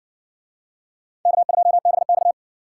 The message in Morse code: K2BX

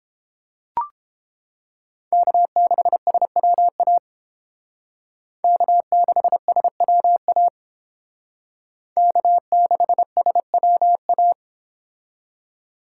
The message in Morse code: K6SWA K6SWA K6SWA

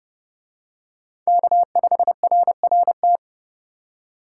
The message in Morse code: K5RRT

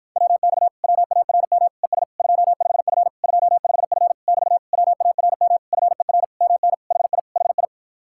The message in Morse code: WX WARM ES 25F 25F <BT> WARM FER MN HI HI

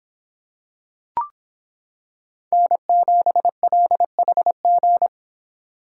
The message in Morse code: N7LHG

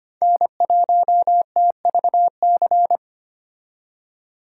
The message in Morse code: N1TVC

